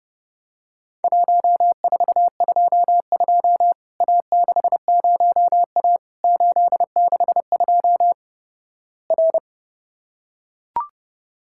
Morse code, heard as a keyed tone, about 30 wpm, 700 Hz. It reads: 1422 A60A 862 R
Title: Worked All Europe DX Contest - QTC Exchange - 30wpm